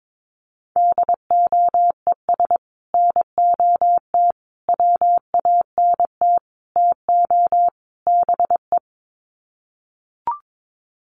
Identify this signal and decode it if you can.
Morse code, heard as a keyed tone, about 22 wpm, 700 Hz. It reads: DOES NOT WANT TO BE
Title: Sets of 5 Words Encore 22wpm